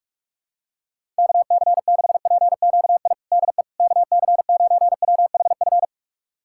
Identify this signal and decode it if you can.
Morse code, heard as a keyed tone, about 41 wpm, 700 Hz. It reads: KC6PQI DE KC9WHF
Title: USA Callsign DE Callsign - 6 Characters - 45wpm